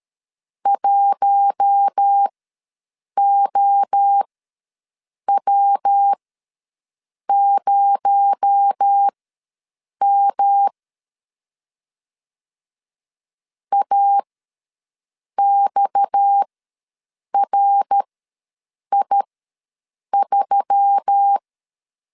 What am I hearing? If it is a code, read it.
Morse code, heard as a keyed tone, about 13 wpm, spaced as for 6 wpm, 800 Hz. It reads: 1OW0M AXRI3
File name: LEZIONI / 6 PKX0 / 6 PKX0 ws 12wpm 3448